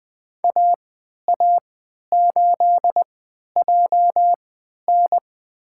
Morse code, heard as a keyed tone, about 20 wpm, spaced as for 11 wpm, 700 Hz. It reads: AA8JN
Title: USA Call Signs - 5 Characters - 3x Character Spacing - 20wpm